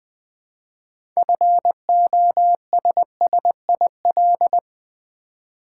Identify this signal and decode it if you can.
Morse code, heard as a keyed tone, about 20 wpm, 700 Hz. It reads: FOSSIL
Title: 6 Letter Words - First 3 Letters Given - 20wpm